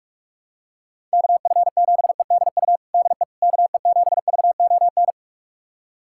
Morse code, heard as a keyed tone, about 46 wpm, 700 Hz. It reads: KF7EDU DE KE7VON
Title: USA Callsign DE Callsign - 6 Characters - 45wpm